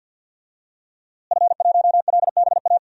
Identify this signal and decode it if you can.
Morse code, heard as a keyed tone, about 52 wpm, 700 Hz. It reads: F1LBA
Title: DXCC-100 Call Signs - 5 Characters - Repeated 2x - 50wpm